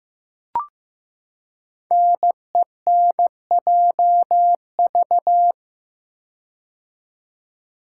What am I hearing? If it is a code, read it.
Morse code, heard as a keyed tone, about 15 wpm, 700 Hz. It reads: NENJV
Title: Code Groups - 5 Letters - 15wpm